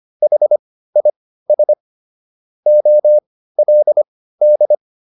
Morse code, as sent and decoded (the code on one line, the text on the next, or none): .... .. ... / --- .-.. -..
HIS OLD